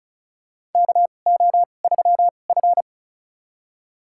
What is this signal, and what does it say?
Morse code, heard as a keyed tone, about 36 wpm, spaced as for 25 wpm, 700 Hz. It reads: KO3F
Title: USA Call Signs - 4 Characters - 2x Character Spacing - 35wpm